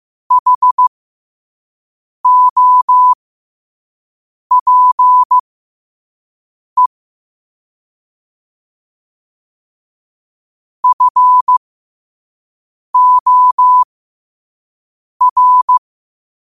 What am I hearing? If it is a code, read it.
Morse code, heard as a keyed tone, about 15 wpm, spaced as for 5 wpm, 1000 Hz. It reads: HOPE FOR
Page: New quotes every day in morse code at 5 Words per minute.